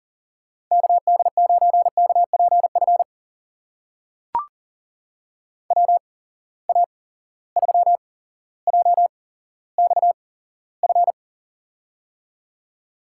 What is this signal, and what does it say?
Morse code, heard as a keyed tone, about 40 wpm, spaced as for 17 wpm, 700 Hz. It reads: KD9KPF WA3JXF
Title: USA Call Signs - 6 Characters - 8x Character Spacing - 40wpm